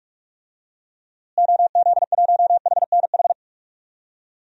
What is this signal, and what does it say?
Morse code, heard as a keyed tone, about 46 wpm, 700 Hz. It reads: OZ1HNH
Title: DXCC-100 Call Signs - 6 Characters - Repeated 3x - 45wpm